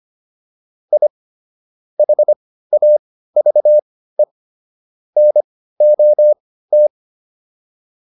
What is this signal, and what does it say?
Morse code, heard as a keyed tone, about 25 wpm, spaced as for 15 wpm, 600 Hz. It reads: I HAVE NOT
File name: WD204 15